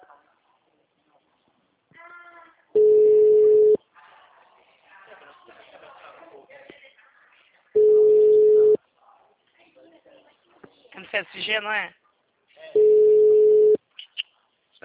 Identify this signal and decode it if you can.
Ringback tone (Europe, 425 Hz), with speech.
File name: out- -2003- - - .5497